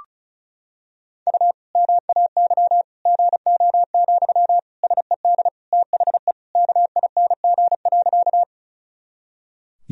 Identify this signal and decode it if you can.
Morse code, heard as a keyed tone, about 34 wpm, 700 Hz. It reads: U MAY GO, SED THE KING.
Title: Sentences from Top 400 Words and Abbreviations 35wpm